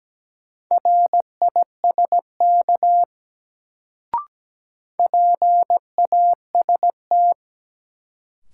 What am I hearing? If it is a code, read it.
Morse code, heard as a keyed tone, about 17 wpm, 700 Hz. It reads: RISK PAST
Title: Sentences from Top 600 Words - New Word Drill - 17wpm